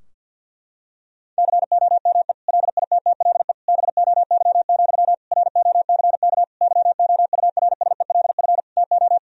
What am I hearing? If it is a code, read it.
Morse code, heard as a keyed tone, about 52 wpm, 700 Hz. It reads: COME LITTLE BOY, ROCK YOURSELF TO
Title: Sentences from Top 1000 Words - Review Part 2 - 50wpm